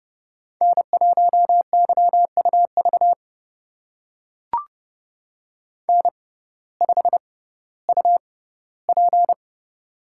Morse code, heard as a keyed tone, about 30 wpm, spaced as for 17 wpm, 700 Hz. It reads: N1YUV N5UP
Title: USA Call Signs - 6x Character Spacing - 30wpm